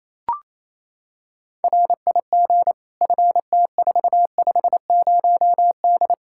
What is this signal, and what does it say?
Morse code, heard as a keyed tone, about 28 wpm, 700 Hz. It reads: RIG FT450D
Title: Practice QSO - Exchange 2 - Info - 28wpm